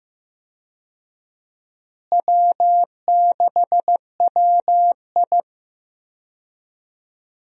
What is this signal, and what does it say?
Morse code, heard as a keyed tone, about 15 wpm, 700 Hz. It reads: W6WI
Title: USA Call Signs - 4 Characters - Repeated 2x - 15wpm